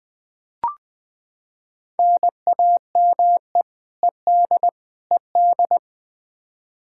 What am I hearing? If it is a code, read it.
Morse code, heard as a keyed tone, about 20 wpm, 700 Hz. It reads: NAME ED ED
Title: Practice QSO - Exchange 1 - Beginning - 20wpm